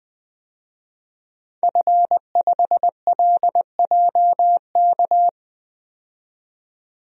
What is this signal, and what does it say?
Morse code, heard as a keyed tone, about 20 wpm, 700 Hz. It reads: F5LJK
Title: DXCC-100 Call Signs - 5 Characters - 20wpm